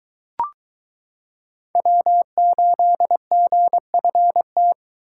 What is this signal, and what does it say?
Morse code, heard as a keyed tone, about 22 wpm, 700 Hz. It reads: W8GFT